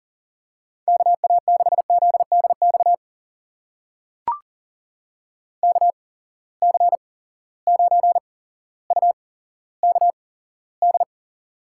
Morse code, heard as a keyed tone, about 42 wpm, spaced as for 17 wpm, 700 Hz. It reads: KA6ZDX KC9UKD
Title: USA Call Signs - 6 Characters - 8x Character Spacing - 40wpm